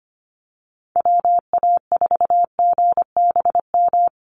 Morse code, heard as a keyed tone, about 25 wpm, 700 Hz. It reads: WA4GBM
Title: USA Call Signs Repeated 3x 25wpm